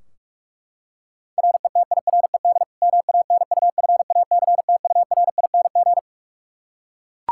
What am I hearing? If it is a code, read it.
Morse code, heard as a keyed tone, about 42 wpm, 700 Hz. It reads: RETIRED MANUFACTURING
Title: Practice QSO - Exchange 3 - More Info - 45wpm